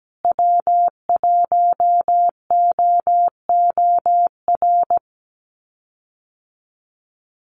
Morse code, heard as a keyed tone, about 17 wpm, 700 Hz. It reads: W1OOR
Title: USA Call Signs Repeated 2x 17wpm